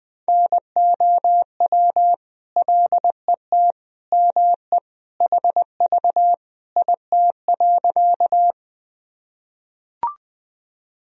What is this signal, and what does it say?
Morse code, heard as a keyed tone, about 20 wpm, 700 Hz. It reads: NOW LET ME HV IT.
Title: Sentences from Top 300 Words and Abbreviations 20wpm